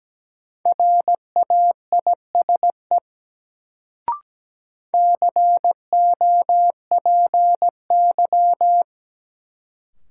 Morse code, heard as a keyed tone, about 17 wpm, 700 Hz. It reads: RAISE COPY
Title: Sentences from Top 1400 Words - New Word Drill - 17wpm